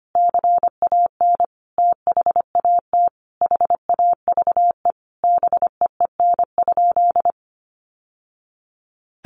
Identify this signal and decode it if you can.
Morse code, heard as a keyed tone, about 25 wpm, 700 Hz. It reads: CAN THAT HAVE BEEN?
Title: Sentences from Top 800 Words 25wpm